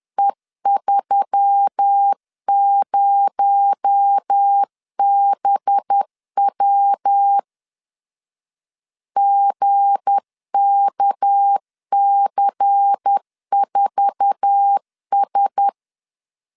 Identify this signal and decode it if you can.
Morse code, heard as a keyed tone, about 10 wpm, 800 Hz. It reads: E30BW GKC4S
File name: LEZIONI / 9 2 10wpm 2736